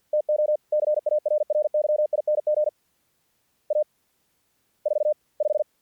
Morse code, heard as a keyed tone, about 50 wpm, spaced as for 44 wpm, 610 Hz. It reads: TO CARRYING A 45
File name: archived / CRSnet